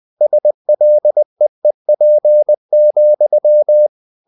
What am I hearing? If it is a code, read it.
Morse code, heard as a keyed tone, about 20 wpm, 600 Hz. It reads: SLEEP,
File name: the old man and the sea part 2 20wpm